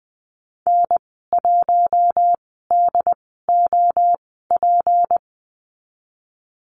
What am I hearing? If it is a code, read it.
Morse code, heard as a keyed tone, about 20 wpm, spaced as for 14 wpm, 700 Hz. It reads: N1DOP
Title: 186 Lesson 30g - Letter b - Call Signs 20wpm